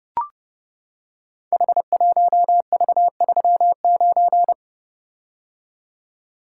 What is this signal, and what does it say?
Morse code, heard as a keyed tone, about 30 wpm, 700 Hz. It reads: H1V39
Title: Code Groups - 5 Letter:Numbers - 30wpm